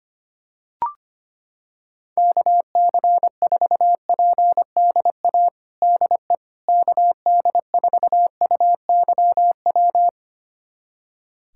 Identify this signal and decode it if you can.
Morse code, heard as a keyed tone, about 25 wpm, 700 Hz. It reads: KC4PDA DE KD4UYW
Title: USA Callsign DE Callsign - 6 Characters - 25wpm